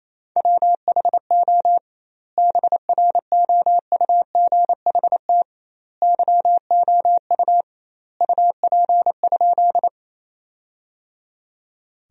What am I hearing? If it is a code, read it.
Morse code, heard as a keyed tone, about 28 wpm, 700 Hz. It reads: WHO BROUGHT YOU UP?
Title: 2x Word-Spacing - Sentences from Top 500 Words - 28wpm